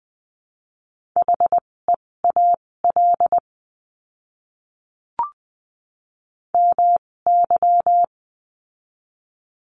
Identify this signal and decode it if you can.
Morse code, heard as a keyed tone, about 20 wpm, spaced as for 16 wpm, 700 Hz. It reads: HEAL MY
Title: 256 Lesson 42a - Words 12-20wpm